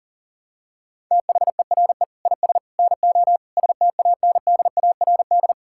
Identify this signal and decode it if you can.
Morse code, heard as a keyed tone, about 40 wpm, 700 Hz. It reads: THERE IS NO STANDARD
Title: Sentences from Top 1000 Words - Review Part 6 - 40wpm